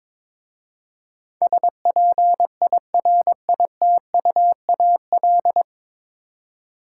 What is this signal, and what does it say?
Morse code, heard as a keyed tone, about 21 wpm, 700 Hz. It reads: SPIRITUAL
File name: 9 Letter Words - First 3 Letters Given - 22wpm